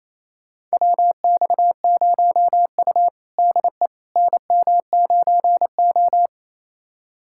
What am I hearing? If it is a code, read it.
Morse code, heard as a keyed tone, about 28 wpm, 700 Hz. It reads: WX0U DE NM9O
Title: USA Callsign DE Callsign - 4 Characters - 28wpm